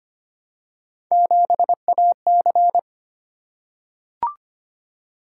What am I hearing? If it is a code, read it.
Morse code, heard as a keyed tone, about 25 wpm, 700 Hz. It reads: 7AC
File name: Three Letter-Number Combinations - 25wpm